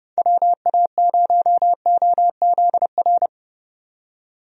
Morse code, heard as a keyed tone, about 31 wpm, 700 Hz. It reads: WA0OZR
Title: USA Call Signs - 6 Characters - 30wpm